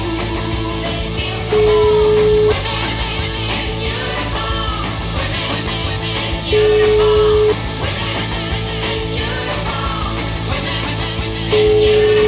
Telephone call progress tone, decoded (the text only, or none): ringback tone